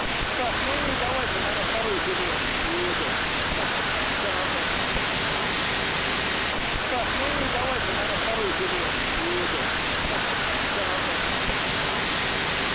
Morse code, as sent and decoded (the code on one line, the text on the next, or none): none